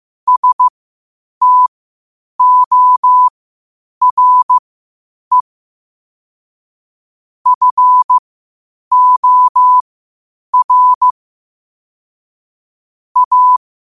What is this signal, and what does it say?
Morse code, heard as a keyed tone, about 15 wpm, spaced as for 8 wpm, 1000 Hz. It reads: STORE FOR A
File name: qotdfiles / 8wpm